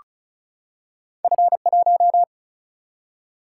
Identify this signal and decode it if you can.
Morse code, heard as a keyed tone, about 36 wpm, 700 Hz. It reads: F1
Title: Single-Letter Single-Number 35wpm